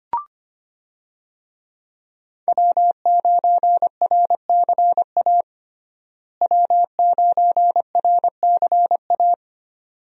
Morse code, heard as a keyed tone, about 25 wpm, 700 Hz. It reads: W9RCA W9RCA
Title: USA Call Signs - 5 Characters - Repeated 2x - 25wpm